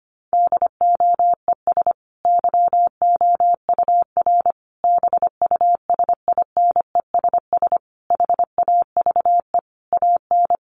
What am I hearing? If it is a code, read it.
Morse code, heard as a keyed tone, about 25 wpm, 700 Hz. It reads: DOES YOUR BUSINESS HAVE AN
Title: Sentences from Top 1200 Words 25wpm